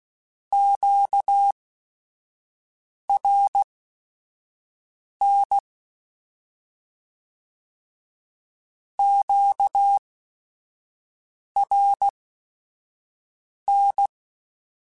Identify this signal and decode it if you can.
Morse code, heard as a keyed tone, about 16 wpm, spaced as for 5 wpm, 800 Hz. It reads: QRN QRN
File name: Q Calls / Q Calls-mp3s / Q Calls Twice @5